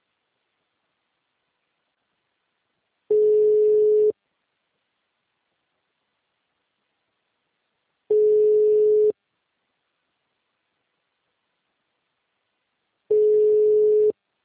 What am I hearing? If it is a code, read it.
Ringback tone (Europe, 425 Hz).
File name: out- -2013- - - .2499